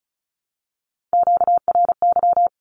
Morse code, heard as a keyed tone, about 34 wpm, 700 Hz. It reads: QRY
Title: Q-Codes 35wpm